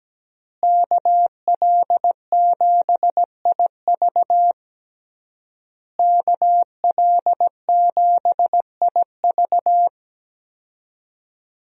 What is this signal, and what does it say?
Morse code, heard as a keyed tone, about 17 wpm, 700 Hz. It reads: KL7IV KL7IV
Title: USA Call Signs - 5 Characters - Repeated 3x - 17wpm